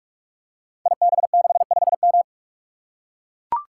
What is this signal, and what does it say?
Morse code, heard as a keyed tone, about 44 wpm, 700 Hz. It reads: IB65M
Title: Code Groups - 5 Letter:Numbers - 45wpm